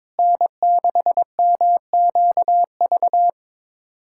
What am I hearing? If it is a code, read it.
Morse code, heard as a keyed tone, about 22 wpm, 700 Hz. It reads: N6MQV
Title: USA Callsign DE Callsign 22wpm